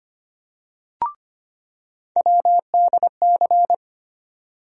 Morse code, heard as a keyed tone, about 25 wpm, 700 Hz. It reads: WDC
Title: USA Call Sign Postfixes - 3 Letters 25wpm